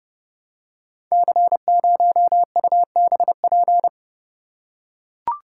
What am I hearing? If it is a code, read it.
Morse code, heard as a keyed tone, about 30 wpm, 700 Hz. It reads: C0UBP
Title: Code Groups - 5 Letter:Numbers - 30wpm